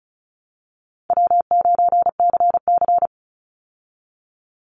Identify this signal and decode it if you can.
Morse code, heard as a keyed tone, about 34 wpm, 700 Hz. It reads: W9CC
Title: Call Signs Repeated 2x - 35wpm